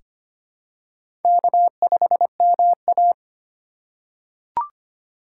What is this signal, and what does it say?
Morse code, heard as a keyed tone, about 25 wpm, 700 Hz. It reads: K5MA
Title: K1USN SST QSO Practice - 25wpm